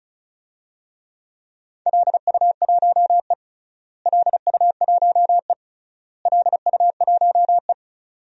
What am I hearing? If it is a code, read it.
Morse code, heard as a keyed tone, about 34 wpm, 700 Hz. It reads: LU1E LU1E LU1E
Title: DXCC-100 Call Signs - 4 Characters - Repeated 3x - 35wpm